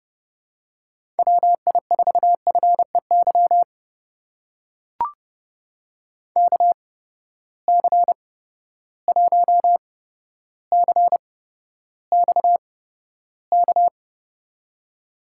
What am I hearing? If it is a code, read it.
Morse code, heard as a keyed tone, about 30 wpm, spaced as for 13 wpm, 700 Hz. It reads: WI4FEY KC1CXK
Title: USA Call Signs - 6 Characters - 8x Character Spacing - 30wpm